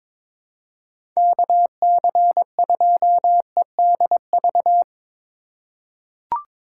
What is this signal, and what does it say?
Morse code, heard as a keyed tone, about 22 wpm, 700 Hz. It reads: KC2EDV
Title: USA Call Signs - 6 Characters - 2x Character Spacing - 22wpm